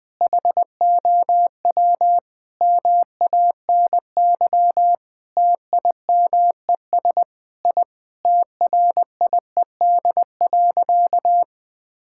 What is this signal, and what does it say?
Morse code, heard as a keyed tone, about 20 wpm, 700 Hz. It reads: HOW MANY TIMES I TRIED.
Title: Sentences with 8-Word Phrases - Spoken Beginning - 20wpm